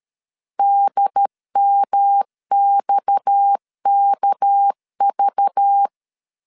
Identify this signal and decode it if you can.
Morse code, heard as a keyed tone, about 13 wpm, 800 Hz. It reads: DMXKV
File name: LEZIONI / 8 YJ45 / 8 YJ 12wpm 2235